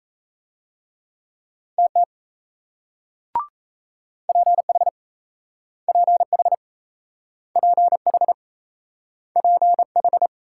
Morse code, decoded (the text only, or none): TT PH PH PH PH